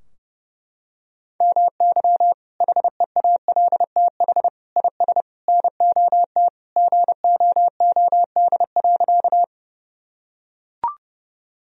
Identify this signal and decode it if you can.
Morse code, heard as a keyed tone, about 32 wpm, 700 Hz. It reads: MY HEALTH IS NOT GOOD.
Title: Sentences from Top 1000 Words - Review Part 3 - 30wpm